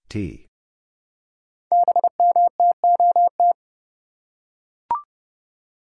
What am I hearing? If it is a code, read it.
Morse code, heard as a keyed tone, about 30 wpm, 700 Hz. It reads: BMTOT
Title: Code Groups - 5 Letters - 30wpm